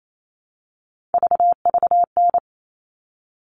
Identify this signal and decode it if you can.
Morse code, heard as a keyed tone, about 28 wpm, 700 Hz. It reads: VVN
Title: All Cut Numbers - 3-Digit 28wpm